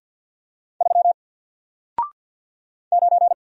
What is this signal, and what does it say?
Morse code, heard as a keyed tone, about 50 wpm, 700 Hz. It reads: 3 9
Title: Single Character (Letter:Number:Punctuation:Prosign) Alt 50wpm